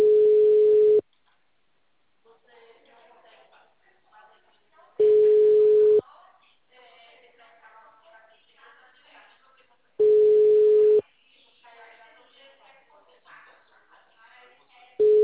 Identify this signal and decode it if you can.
Ringback tone (Europe, 425 Hz), with speech.